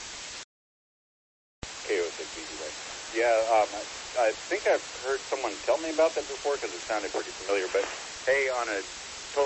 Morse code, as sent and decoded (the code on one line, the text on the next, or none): none